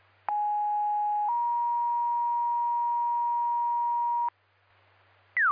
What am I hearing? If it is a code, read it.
Two-tone page, heard as a one-second tone, then a three-second tone, about 830 then 950 Hz. It is heard from 0.3 to 4.3 s.